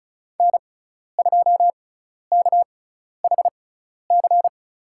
Morse code, heard as a keyed tone, about 36 wpm, spaced as for 12 wpm, 700 Hz. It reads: N2KHC
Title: USA Call Signs - 5 Characters - 6x Character Spacing - 35wpm